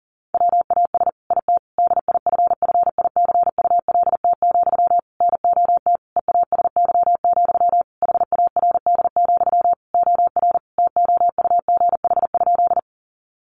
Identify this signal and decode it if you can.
Morse code, heard as a keyed tone, about 40 wpm, 700 Hz. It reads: WAS IT DIFFICULT, NOT EASY, HARD, OR TOUGH?
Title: Sentences from Top 1200 Words 40wpm